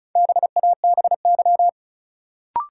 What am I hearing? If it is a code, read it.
Morse code, heard as a keyed tone, about 34 wpm, 700 Hz. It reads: BABY
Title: Sentences from Top 1300 Words - New Word Drill Alt - 35wpm